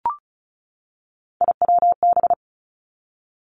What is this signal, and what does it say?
Morse code, heard as a keyed tone, about 36 wpm, 700 Hz. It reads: IWB
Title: Three Challenging Letter-Number Combinations 35wpm